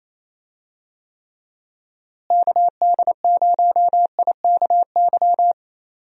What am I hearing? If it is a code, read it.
Morse code, heard as a keyed tone, about 28 wpm, 700 Hz. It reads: KD0IKY
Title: USA Call Signs - 6 Characters - Repeated 3x - 28wpm